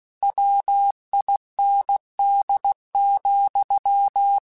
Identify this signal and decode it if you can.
Morse code, heard as a keyed tone, about 16 wpm, 800 Hz. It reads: WIND,
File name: Weather Sets / Weather Sets-mp3s / Weather 4 17 @15